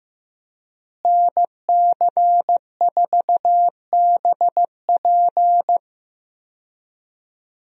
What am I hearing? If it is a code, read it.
Morse code, heard as a keyed tone, about 15 wpm, 700 Hz. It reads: NC4BP
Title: USA Call Signs - 5 Characters - Repeated 2x - 15wpm